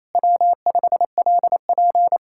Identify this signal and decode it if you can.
Morse code, heard as a keyed tone, about 32 wpm, 700 Hz. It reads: W5LP
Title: Speed-Racing Common QSO Phrases - 28wpm